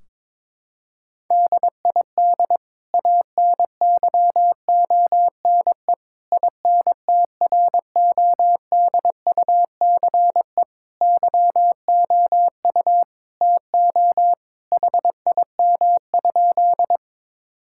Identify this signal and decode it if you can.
Morse code, heard as a keyed tone, about 21 wpm, 700 Hz. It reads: DID ANYONE INTRODUCE YOU TO HIM?
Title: Sentences from Top 1600 Words 22wpm